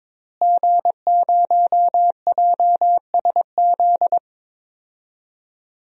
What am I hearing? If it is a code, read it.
Morse code, heard as a keyed tone, about 22 wpm, 700 Hz. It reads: G0JSZ
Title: DXCC-100 Call Signs - 5 Characters - Repeated 2x - 22wpm